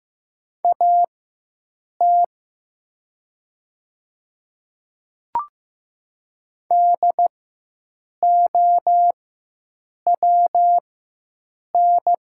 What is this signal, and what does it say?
Morse code, heard as a keyed tone, about 15 wpm, spaced as for 7 wpm, 700 Hz. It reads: AT DOWN
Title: 4x Character Spacing - Top 100 Words - 15wpm